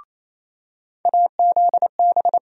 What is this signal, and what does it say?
Morse code, heard as a keyed tone, about 28 wpm, 700 Hz. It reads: AZB
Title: USA Call Sign Postfixes - 3 Letters 28wpm